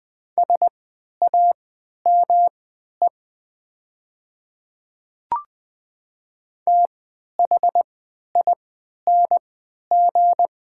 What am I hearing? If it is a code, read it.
Morse code, heard as a keyed tone, about 20 wpm, spaced as for 11 wpm, 700 Hz. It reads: SAME THING
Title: 3x Character Spacing - Top 200 Words - 20wpm